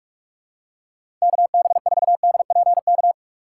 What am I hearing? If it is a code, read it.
Morse code, heard as a keyed tone, about 46 wpm, 700 Hz. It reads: KB4DPK
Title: USA Call Signs - 6 Characters - 3x Character Spacing - 45wpm